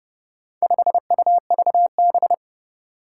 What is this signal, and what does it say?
Morse code, heard as a keyed tone, about 30 wpm, 700 Hz. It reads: 5UVB
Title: Four Challenging Letter-Number Combinations 30wpm